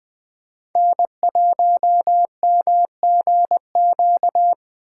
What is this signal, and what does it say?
Morse code, heard as a keyed tone, about 18 wpm, 700 Hz. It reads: N1MGQ
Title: Speed Racing - USA 1x3 Call Signs 15wpm